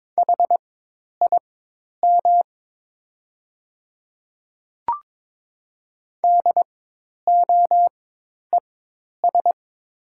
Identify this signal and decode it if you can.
Morse code, heard as a keyed tone, about 22 wpm, spaced as for 10 wpm, 700 Hz. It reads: HIM DOES
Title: 4x Character Spacing - Top 200 Words - 22wpm